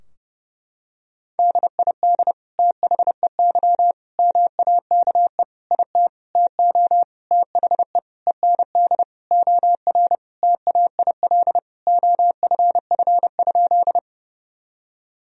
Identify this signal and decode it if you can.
Morse code, heard as a keyed tone, about 28 wpm, 700 Hz. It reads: DID THEY MAKE IT TO THE END OR TAIL OFF?
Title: Sentences from Top 1000 Words - Review Part 3 - 30wpm